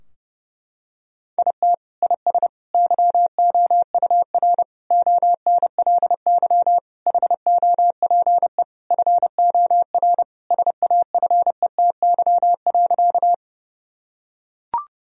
Morse code, heard as a keyed tone, about 32 wpm, 700 Hz. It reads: IT IS YOUR ONLY HOPE FOR SAFETY.
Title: Sentences from Top 1000 Words - Review Part 4 - 30wpm